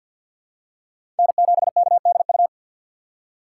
Morse code, heard as a keyed tone, about 52 wpm, 700 Hz. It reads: N7KDU
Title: USA Call Signs - 5 Characters - 3x Character Spacing - 50wpm